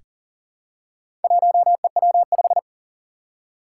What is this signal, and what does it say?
Morse code, heard as a keyed tone, about 43 wpm, 700 Hz. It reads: 1EW5
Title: Four Challenging Letter-Number Combinations 40wpm